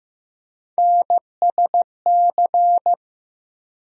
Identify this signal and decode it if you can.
Morse code, heard as a keyed tone, about 15 wpm, 700 Hz. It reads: NSC
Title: USA Call Sign Postfixes - 3 Letters 15wpm